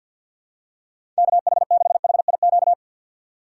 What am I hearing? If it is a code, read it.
Morse code, heard as a keyed tone, about 49 wpm, 700 Hz. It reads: KH6HIQ
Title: USA Call Signs - 6 Characters - 6x Character Spacing - 50wpm